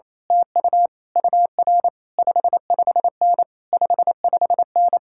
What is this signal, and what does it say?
Morse code, heard as a keyed tone, about 26 wpm, 700 Hz. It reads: TU UR 55N 55N
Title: Speed-Racing Common QSO Phrases - 28wpm